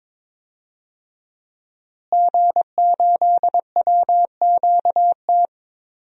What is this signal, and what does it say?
Morse code, heard as a keyed tone, about 22 wpm, 700 Hz. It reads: G8WQT
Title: DXCC-100 Call Signs - 5 Characters - Repeated 2x - 22wpm